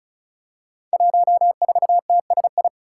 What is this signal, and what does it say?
Morse code, heard as a keyed tone, about 36 wpm, 700 Hz. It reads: 14TSI
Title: Five Challenging Letter-Number Combinations 35wpm